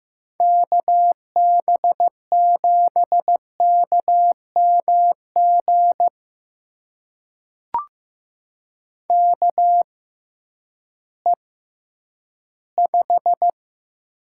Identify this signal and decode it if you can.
Morse code, heard as a keyed tone, about 15 wpm, spaced as for 10 wpm, 700 Hz. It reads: KB7KMG KE5
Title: USA Call Signs - 6 Characters - 6x Character Spacing - 15wpm